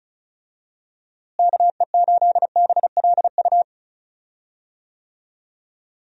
Morse code, heard as a keyed tone, about 36 wpm, 700 Hz. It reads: KE8BLU
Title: USA Call Signs - 6 Characters - Repeated 3x - 35wpm